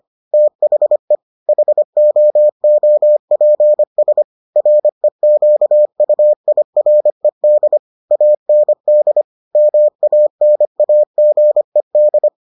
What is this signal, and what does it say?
Morse code, heard as a keyed tone, about 25 wpm, 600 Hz. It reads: THE HOOPS REQUIRED AND MANAGED